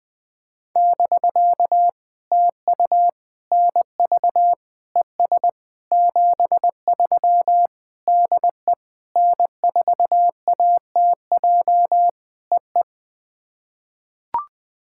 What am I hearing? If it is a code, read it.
Morse code, heard as a keyed tone, about 20 wpm, 700 Hz. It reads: <BK> TU NV ES 73 DE N4ATJ EE
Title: POTA QSO Practice - 20wpm